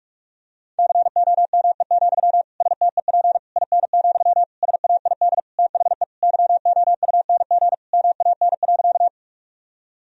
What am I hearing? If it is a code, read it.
Morse code, heard as a keyed tone, about 46 wpm, 700 Hz. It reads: COME, STEP IN, SAID THE YOUNG MAN.
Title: Sentences from Top 1000 Words - Review Part 2 - 45wpm